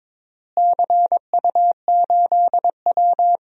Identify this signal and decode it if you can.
Morse code, heard as a keyed tone, about 22 wpm, 700 Hz. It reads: CU8W
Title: DXCC-100 Call Signs - 4 Characters - Repeated 3x - 22wpm